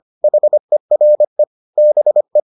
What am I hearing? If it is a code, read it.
Morse code, heard as a keyed tone, about 25 wpm, 600 Hz. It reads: HERE BE